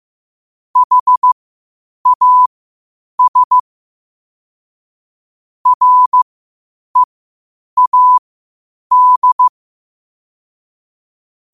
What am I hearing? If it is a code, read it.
Morse code, heard as a keyed tone, about 15 wpm, spaced as for 8 wpm, 1000 Hz. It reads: HAS READ
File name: qotdfiles / 8wpm